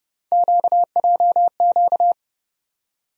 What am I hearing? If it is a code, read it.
Morse code, heard as a keyed tone, about 31 wpm, 700 Hz. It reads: QJQ
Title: USA Call Sign Postfixes - 3 Letters 30wpm